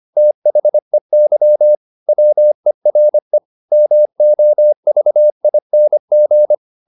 Morse code, heard as a keyed tone, about 25 wpm, 600 Hz. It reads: THEY WERE MOVING
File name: the old man and the sea part 1 25wpm